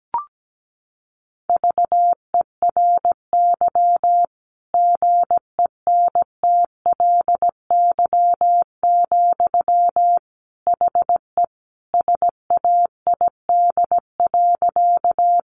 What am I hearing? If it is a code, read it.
Morse code, heard as a keyed tone, about 17 wpm, 700 Hz. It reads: VERY GENTLY, HE SAID.
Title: Sentences from Top 1500 Words 17wpm